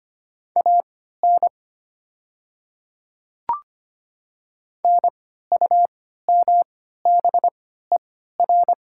Morse code, heard as a keyed tone, about 25 wpm, spaced as for 14 wpm, 690 Hz. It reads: AN NUMBER
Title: 3x Character Spacing - Top 100 Words - 25wpm